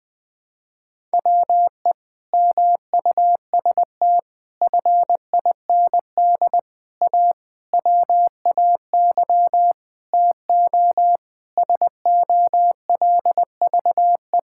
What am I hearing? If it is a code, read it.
Morse code, heard as a keyed tone, about 21 wpm, 700 Hz. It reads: WE MUST FIND A WAY TO SOLVE
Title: Sentences with 4-Word Phrases - Spoken Beginning - 20wpm